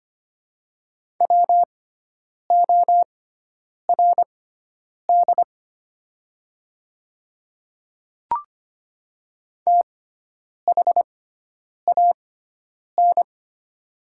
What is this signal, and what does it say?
Morse code, heard as a keyed tone, about 25 wpm, spaced as for 9 wpm, 700 Hz. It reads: WORD THAN
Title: 6x Character Spacing - Top 100 Words - 25wpm